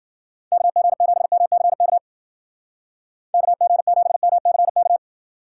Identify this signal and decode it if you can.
Morse code, heard as a keyed tone, about 49 wpm, 700 Hz. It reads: KG7MCK KG7MCK
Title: Speed Racing - USA 2x3 Call Signs 40wpm